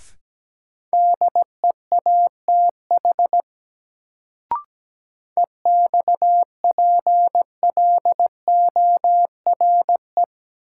Morse code, heard as a keyed tone, about 17 wpm, 700 Hz. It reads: DEATH EXPLORE
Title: Sentences from Top 1100 Words - New Word Drill - 17wpm